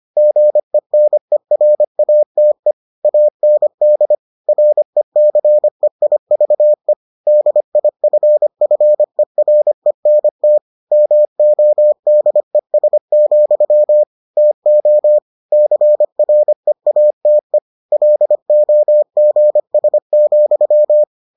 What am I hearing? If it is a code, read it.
Morse code, heard as a keyed tone, about 25 wpm, 600 Hz. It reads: GENERATE AND RECEIVE DIFFERENT MODES, TO CREATE LOGS,